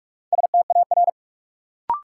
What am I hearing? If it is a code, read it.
Morse code, heard as a keyed tone, about 46 wpm, 700 Hz. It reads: STAR
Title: Sentences from Top 1000 Words - Review Part 2 - New Word Drill - 45wpm